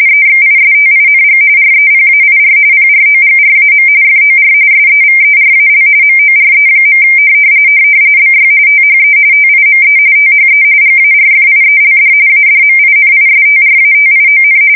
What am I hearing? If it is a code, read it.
RTTY, about 45.45 baud. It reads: EXISTING REPAIR PROGRAM FOR SEVERAL MORE. BY THE TIME THE GARFIELD ADMINISTRATION
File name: fsk170 / usnavy3